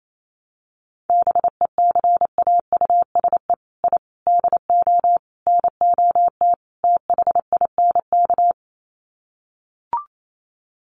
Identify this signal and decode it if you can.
Morse code, heard as a keyed tone, about 28 wpm, 700 Hz. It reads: BECAUSE I DO NOT THINK
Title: Sets of 5 Words Encore 28wpm